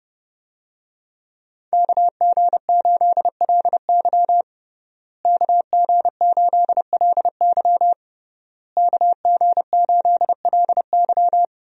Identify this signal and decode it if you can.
Morse code, heard as a keyed tone, about 30 wpm, 700 Hz. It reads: KG8LY KG8LY KG8LY
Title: USA Call Signs - 5 Characters - Repeated 3x - 30wpm